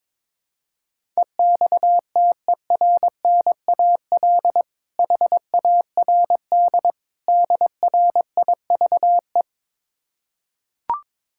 Morse code, heard as a keyed tone, about 22 wpm, 700 Hz. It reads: EXTERNAL HARD DRIVE
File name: Difficult Words with Hints - 22wpm